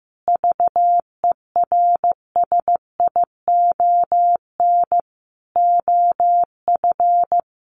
Morse code, heard as a keyed tone, about 15 wpm, 700 Hz. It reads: VERSION OF
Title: Sentences from Top 1400 Words 15wpm